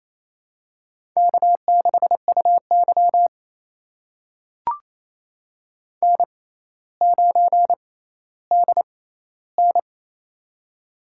Morse code, heard as a keyed tone, about 28 wpm, spaced as for 14 wpm, 700 Hz. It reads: K6UY N9DN
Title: USA Call Signs - 4 Characters - 6x Character Spacing - 28wpm